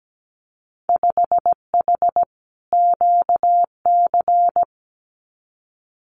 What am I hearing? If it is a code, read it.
Morse code, heard as a keyed tone, about 17 wpm, 700 Hz. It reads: 5H QC
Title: Winter Field Day Exchanges Encore 17wpm